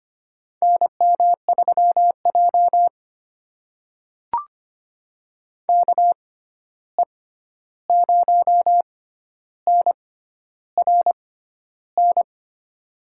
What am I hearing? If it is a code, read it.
Morse code, heard as a keyed tone, about 25 wpm, spaced as for 11 wpm, 700 Hz. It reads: NM3J KE0NRN
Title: USA Call Signs - 6x Character Spacing - 25wpm